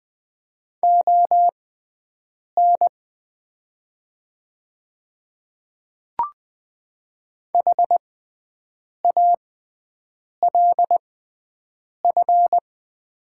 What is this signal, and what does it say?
Morse code, heard as a keyed tone, about 20 wpm, spaced as for 7 wpm, 700 Hz. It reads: ON HALF
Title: 6x Character Spacing - Top 500 Words - 20wpm